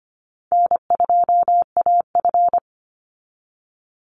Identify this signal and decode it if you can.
Morse code, heard as a keyed tone, about 25 wpm, 700 Hz. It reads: N2AF
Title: NA SOTA Chaser Callsigns 25wpm